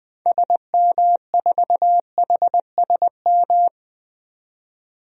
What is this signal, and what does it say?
Morse code, heard as a keyed tone, about 20 wpm, 700 Hz. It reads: SM4HSM
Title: DXCC-100 Call Signs - 6 Characters - 20wpm